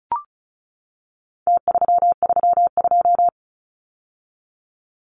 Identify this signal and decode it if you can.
Morse code, heard as a keyed tone, about 34 wpm, 700 Hz. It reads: T332
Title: Numbers - Contest Style - 4-Digit 35wpm